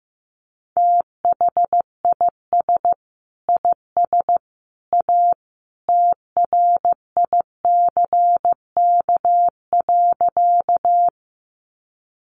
Morse code, heard as a keyed tone, about 15 wpm, 700 Hz. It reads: THIS IS A TRICK.
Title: Sentences from Top 1500 Words 15wpm